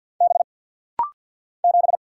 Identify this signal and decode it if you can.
Morse code, heard as a keyed tone, about 50 wpm, 700 Hz. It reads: B 7
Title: Single Letter-Number - Rapid-Fire 50wpm